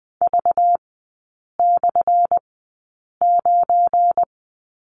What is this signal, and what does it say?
Morse code, heard as a keyed tone, about 20 wpm, spaced as for 8 wpm, 700 Hz. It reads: V/9
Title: 191 Lesson 31e - Letter v - Three Characters 20wpm